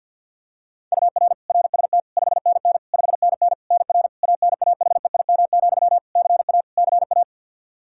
Morse code, heard as a keyed tone, about 53 wpm, 700 Hz. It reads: UR RST 5NN 5NN NR ANAHEIM, CA CA
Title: Practice QSO - Exchange 1 - Beginning - 50wpm